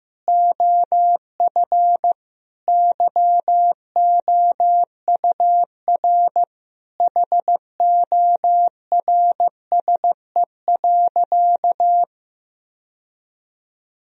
Morse code, heard as a keyed tone, about 15 wpm, 700 Hz. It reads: OF YOUR HORSE.
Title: Sentences from Top 1000 Words - Review Part 2 - 15wpm